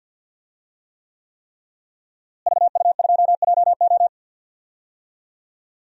Morse code, heard as a keyed tone, about 51 wpm, 700 Hz. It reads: VU2JO
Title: DXCC-100 Call Signs - 5 Characters - Repeated 3x - 50wpm